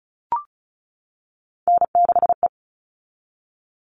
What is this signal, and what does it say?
Morse code, heard as a keyed tone, about 34 wpm, 700 Hz. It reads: N6E
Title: All Cut Numbers - 3-Digit 35wpm